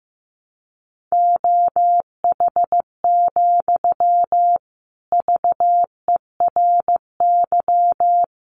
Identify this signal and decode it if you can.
Morse code, heard as a keyed tone, about 15 wpm, 700 Hz. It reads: OH, VERY
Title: Sentences from Top 1000 Words 15wpm